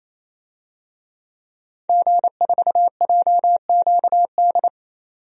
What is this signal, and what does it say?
Morse code, heard as a keyed tone, about 28 wpm, 700 Hz. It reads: G4JQD
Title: DXCC-100 Call Signs - 5 Characters - 28wpm